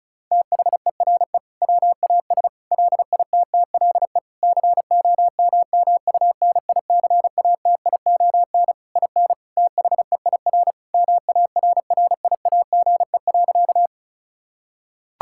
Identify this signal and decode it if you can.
Morse code, heard as a keyed tone, about 34 wpm, 700 Hz. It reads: THERE WAS LITTLE COMMUNICATION IN THEIR MARRIAGE.
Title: Sentences from Top 1000 Words - Review Part 6 - 35wpm